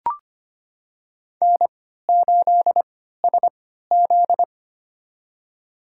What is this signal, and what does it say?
Morse code, heard as a keyed tone, about 25 wpm, spaced as for 14 wpm, 700 Hz. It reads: N8SZ
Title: USA Call Signs - 4 Characters - 3x Character Spacing - 25wpm